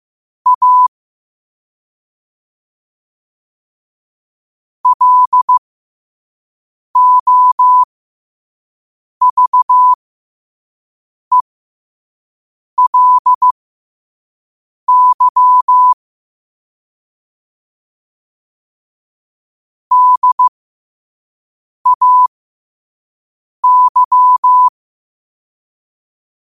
Morse code, heard as a keyed tone, about 15 wpm, spaced as for 5 wpm, 1000 Hz. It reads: A LOVELY DAY
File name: qotdfiles / 5wpm